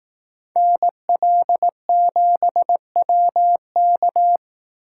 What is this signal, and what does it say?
Morse code, heard as a keyed tone, about 18 wpm, 700 Hz. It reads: NL7WK
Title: Speed Racing - USA 2x2 Call Signs 15wpm